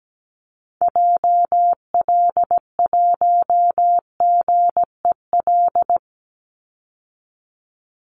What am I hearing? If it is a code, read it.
Morse code, heard as a keyed tone, about 17 wpm, 700 Hz. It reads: JL1GEL
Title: Call Signs Repeated 3x - 17wpm